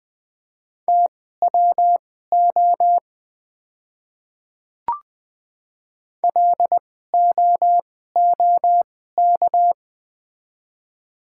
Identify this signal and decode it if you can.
Morse code, heard as a keyed tone, about 21 wpm, spaced as for 15 wpm, 700 Hz. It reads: TWO LOOK
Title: 2x Character Spacing - Top 200 Words - 20wpm